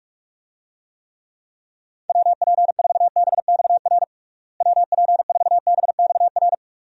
Morse code, heard as a keyed tone, about 46 wpm, 700 Hz. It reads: WP4BXR WP4BXR
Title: USA Call Signs - 6 Characters - Repeated 2x - 45wpm